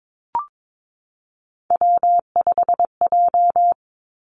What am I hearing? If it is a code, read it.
Morse code, heard as a keyed tone, about 22 wpm, 700 Hz. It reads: W5J
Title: Three Challenging Letter-Number Combinations 22wpm